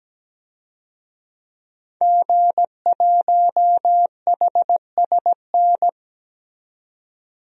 Morse code, heard as a keyed tone, about 17 wpm, 700 Hz. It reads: G1HSN
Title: DXCC-100 Call Signs - 5 Characters - 17wpm